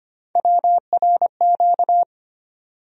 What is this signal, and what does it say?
Morse code, heard as a keyed tone, about 25 wpm, 700 Hz. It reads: WRQ
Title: USA Call Sign Postfixes - 3 Letters 25wpm